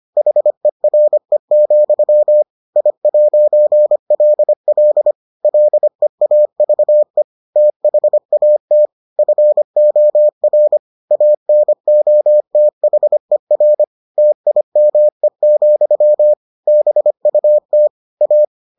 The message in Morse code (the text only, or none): HERE, I'LL LEAVE THAT FOR ANOTHER TIME, BUT A